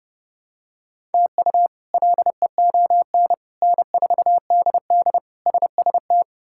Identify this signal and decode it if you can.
Morse code, heard as a keyed tone, about 30 wpm, 700 Hz. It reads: TU LEON N4DD SST
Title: K1USN SST QSO Practice - 30wpm